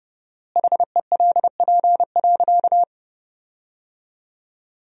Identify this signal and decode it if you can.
Morse code, heard as a keyed tone, about 30 wpm, 700 Hz. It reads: HELP.
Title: 4x Word-Spacing - Sentences from Top 300 Words - 30wpm